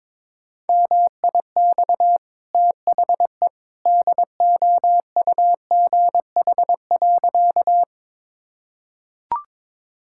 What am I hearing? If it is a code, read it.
Morse code, heard as a keyed tone, about 21 wpm, 690 Hz. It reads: MIX THE DOUGH.
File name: practice / 22wpm / Easy Sentences with Hints - 22wpm